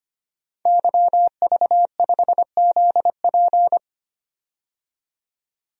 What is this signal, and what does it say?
Morse code, heard as a keyed tone, about 25 wpm, 700 Hz. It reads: YV5ZP
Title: DXCC-100 Call Signs - 5 Characters - 25wpm